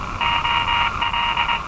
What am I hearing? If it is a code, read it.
Morse code, heard as a keyed tone, about 21 wpm, 950 Hz, with noise. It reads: OL
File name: attachments / RV3GM EW1CY